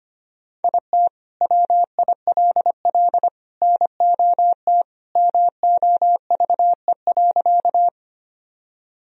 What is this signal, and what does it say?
Morse code, heard as a keyed tone, about 25 wpm, 700 Hz. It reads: IT WILL NOT MOVE.
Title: Sentences from Top 1000 Words - Review Part 1 - 25wpm